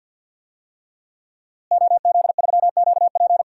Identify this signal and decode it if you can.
Morse code, heard as a keyed tone, about 50 wpm, 700 Hz. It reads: OZ3QP
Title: DXCC-100 Call Signs - 5 Characters - Repeated 3x - 50wpm